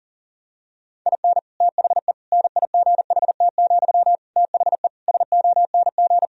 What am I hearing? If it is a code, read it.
Morse code, heard as a keyed tone, about 40 wpm, 700 Hz. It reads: IN THE NIGHT, THE SONG
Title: Sentences from Top 1000 Words - Review Part 2 - 40wpm